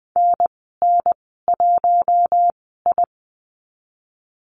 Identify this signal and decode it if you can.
Morse code, heard as a keyed tone, about 19 wpm, spaced as for 14 wpm, 700 Hz. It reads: NN1I
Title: 055 Lesson 10g - Letter r - Call Signs 20wpm